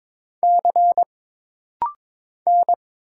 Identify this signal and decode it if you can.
Morse code, heard as a keyed tone, about 22 wpm, 700 Hz. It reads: C N
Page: This rapid-fire timing is geared to push your ICR (Instant Character Recognition) skills.